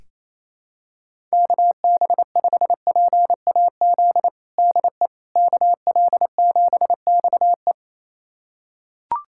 Morse code, heard as a keyed tone, about 28 wpm, 700 Hz. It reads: KB5PAZ DE KL7XE
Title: USA Callsign DE Callsign 28wpm